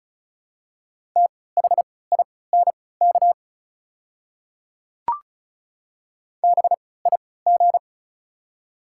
Morse code, heard as a keyed tone, about 34 wpm, spaced as for 20 wpm, 700 Hz. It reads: THINK BIG
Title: 3x Character Spacing - Top 300 Words - 35wpm